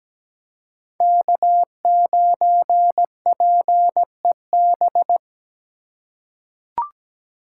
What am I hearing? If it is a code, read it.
Morse code, heard as a keyed tone, about 17 wpm, 700 Hz. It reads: K9PEB
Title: USA Call Signs - 5 Characters - 3x Character Spacing - 17wpm